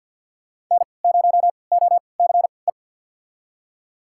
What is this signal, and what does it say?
Morse code, heard as a keyed tone, about 52 wpm, spaced as for 28 wpm, 700 Hz. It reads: N0OCE
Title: USA Call Signs - 5 Characters - 3x Character Spacing - 50wpm